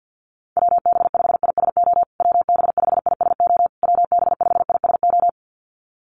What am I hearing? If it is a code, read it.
Morse code, heard as a keyed tone, about 50 wpm, 700 Hz. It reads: WB5ISO WB5ISO WB5ISO
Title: USA Call Signs Repeated 3x 50wpm